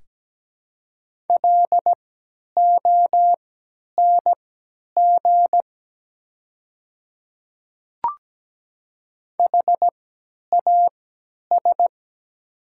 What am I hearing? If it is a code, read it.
Morse code, heard as a keyed tone, about 17 wpm, spaced as for 10 wpm, 700 Hz. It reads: LONG HAS
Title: 3x Character Spacing - Top 500 Words - 17wpm